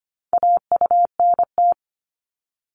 Morse code, heard as a keyed tone, about 25 wpm, 700 Hz. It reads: AUNT
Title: Multi-letter Phonemes 25wpm